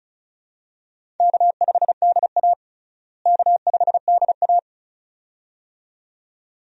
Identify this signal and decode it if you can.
Morse code, heard as a keyed tone, about 36 wpm, 700 Hz. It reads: K5DA K5DA
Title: USA Call Signs - 4 Characters - Repeated 2x - 35wpm